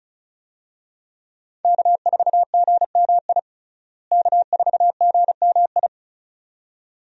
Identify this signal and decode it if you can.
Morse code, heard as a keyed tone, about 36 wpm, 700 Hz. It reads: K4GMI K4GMI
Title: USA Call Signs - 5 Characters - Repeated 2x - 35wpm